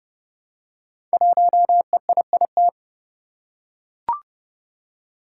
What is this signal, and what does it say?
Morse code, heard as a keyed tone, about 30 wpm, 700 Hz. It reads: 1EIIT
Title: Five Challenging Letter-Number Combinations 30wpm